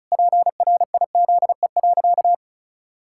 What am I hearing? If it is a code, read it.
Morse code, heard as a keyed tone, about 34 wpm, 700 Hz. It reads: PRIZE.
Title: Pangram Challenge v2 - 35wpm